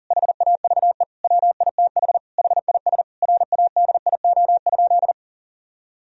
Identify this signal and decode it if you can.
Morse code, heard as a keyed tone, about 40 wpm, 700 Hz. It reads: HAVE WITH HIS RADIO?
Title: Sentences from Top 1000 Words - Review Part 3 - 40wpm